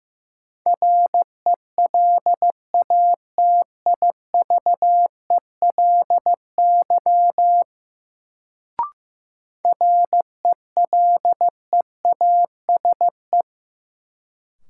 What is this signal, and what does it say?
Morse code, heard as a keyed tone, about 15 wpm, 700 Hz. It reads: RELATIVELY RELEASE
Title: Sentences from Top 1100 Words - New Word Drill Alt - 15wpm